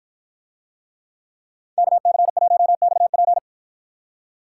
Morse code, heard as a keyed tone, about 49 wpm, 700 Hz. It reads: KC1XP